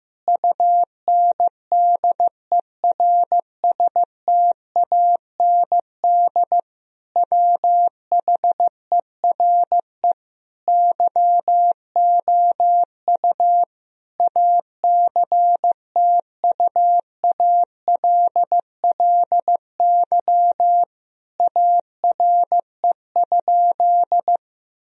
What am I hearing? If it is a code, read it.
Morse code, heard as a keyed tone, about 16 wpm, 700 Hz. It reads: UNDERSTAND WHERE YOU ACTUALLY ARE?
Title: Sentences from Top 1000 Words - Review Part 4 - 15wpm